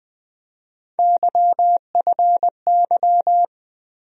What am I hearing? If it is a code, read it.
Morse code, heard as a keyed tone, about 20 wpm, 700 Hz. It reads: YFY
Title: USA Call Sign Postfixes - 3 Letters 20wpm